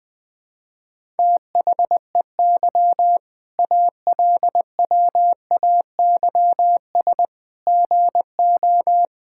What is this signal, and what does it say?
Morse code, heard as a keyed tone, about 21 wpm, 700 Hz. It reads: THEY ALWAYS GO
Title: Sentences with 6-Word Phrases - Spoken Beginning - 20wpm